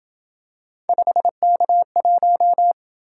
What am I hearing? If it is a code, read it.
Morse code, heard as a keyed tone, about 26 wpm, 700 Hz. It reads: 5K1